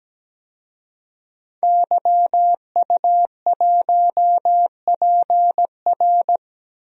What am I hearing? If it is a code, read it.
Morse code, heard as a keyed tone, about 17 wpm, 700 Hz. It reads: YU1PR
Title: DXCC-100 Call Signs - 5 Characters - Repeated 2x - 17wpm